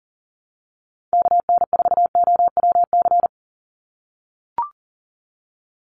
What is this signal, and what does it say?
Morse code, heard as a keyed tone, about 37 wpm, 690 Hz. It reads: KN4OWC
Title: USA Call Signs 40wpm